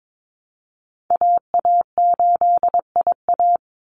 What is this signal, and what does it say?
Morse code, heard as a keyed tone, about 22 wpm, 700 Hz. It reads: AA8IA
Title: Call Signs Repeated 2x - 22wpm